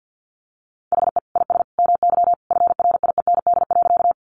Morse code, heard as a keyed tone, about 50 wpm, 700 Hz. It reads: HE IS MY FRIEND.
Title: Sentences from Top 300 Words 50wpm